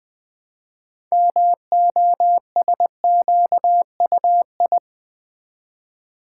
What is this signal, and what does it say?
Morse code, heard as a keyed tone, about 20 wpm, 700 Hz. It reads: MOSQUI
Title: Build A Word - 2 Letters At A Time - 8-Letter Words - 20wpm